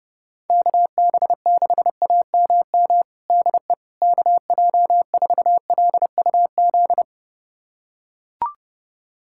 Morse code, heard as a keyed tone, about 30 wpm, 700 Hz. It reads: KB6AMM DE KJ4LUZ
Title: USA Callsign DE Callsign - 6 Characters - 30wpm